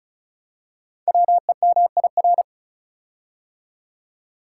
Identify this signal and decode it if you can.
Morse code, heard as a keyed tone, about 36 wpm, 700 Hz. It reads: WEMIR
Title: Code Groups - 5 Letters - 35wpm